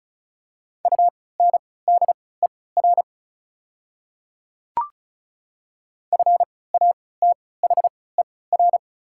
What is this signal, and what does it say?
Morse code, heard as a keyed tone, about 34 wpm, spaced as for 20 wpm, 690 Hz. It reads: UNDER FATHER
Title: 3x Character Spacing - Top 200 Words - 35wpm